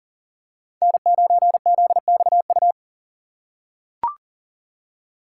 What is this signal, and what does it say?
Morse code, heard as a keyed tone, about 39 wpm, 700 Hz. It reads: N9ZXU
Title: USA Call Signs - 5 Characters - 4x Character Spacing - 40wpm